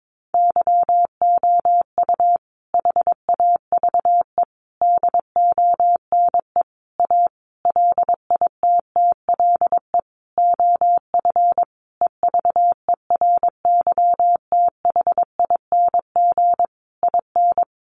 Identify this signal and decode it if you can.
Morse code, heard as a keyed tone, about 22 wpm, 700 Hz. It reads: YOU HAVE DONE A LITTLE OF EVERYTHING IN
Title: Sentences from Top 700 Words 22wpm